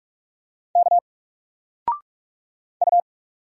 Morse code, heard as a keyed tone, about 44 wpm, 700 Hz. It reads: K U
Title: Single Letter-Number 45wpm